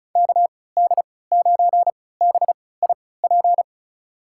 Morse code, heard as a keyed tone, about 36 wpm, spaced as for 20 wpm, 700 Hz. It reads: KD9BIP
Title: USA Call Signs - 6 Characters - 3x Character Spacing - 35wpm